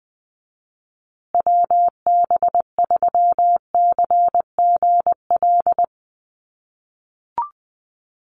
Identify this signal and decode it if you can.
Morse code, heard as a keyed tone, about 20 wpm, 700 Hz. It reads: WB3CGL
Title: USA Call Signs Repeated 2x 20wpm